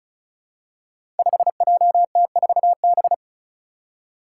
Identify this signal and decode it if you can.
Morse code, heard as a keyed tone, about 34 wpm, 700 Hz. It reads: 5JT4B
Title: Five Challenging Letter-Number Combinations 35wpm